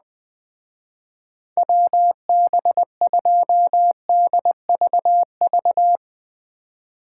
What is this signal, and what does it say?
Morse code, heard as a keyed tone, about 20 wpm, 700 Hz. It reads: WB2DVV